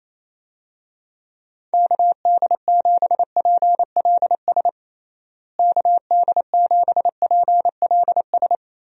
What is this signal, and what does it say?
Morse code, heard as a keyed tone, about 28 wpm, 700 Hz. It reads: KD7PLS KD7PLS
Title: USA Call Signs - 6 Characters - Repeated 2x - 28wpm